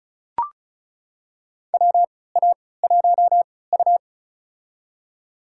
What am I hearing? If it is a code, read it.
Morse code, heard as a keyed tone, about 34 wpm, spaced as for 20 wpm, 700 Hz. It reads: WA1U
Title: USA Call Signs - 4 Characters - 3x Character Spacing - 35wpm